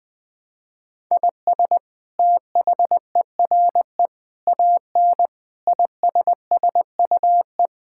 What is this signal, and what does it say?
Morse code, heard as a keyed tone, about 20 wpm, 700 Hz. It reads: IS THERE AN ISSUE
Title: Sentences from Top 1000 Words - Review Part 6 - 20wpm